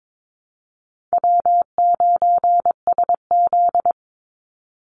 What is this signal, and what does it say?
Morse code, heard as a keyed tone, about 22 wpm, 700 Hz. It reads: W9SZ
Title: Easy Call Signs - 22wpm